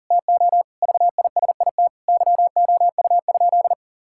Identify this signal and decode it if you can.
Morse code, heard as a keyed tone, about 40 wpm, 700 Hz. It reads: TO VISIT YOU?
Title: Sentences from Top 1000 Words - Review Part 6 - 40wpm